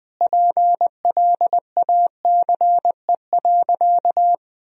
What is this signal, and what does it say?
Morse code, heard as a keyed tone, about 20 wpm, 700 Hz. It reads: PLACE.
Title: Sentences from Top 1000 Words - Review Part 1 - 20wpm